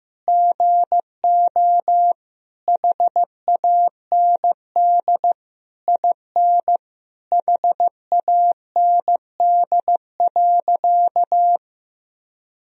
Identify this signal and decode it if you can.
Morse code, heard as a keyed tone, about 15 wpm, 700 Hz. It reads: GO HAND IN HAND.
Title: Sentences from Top 1000 Words - Review Part 4 - 15wpm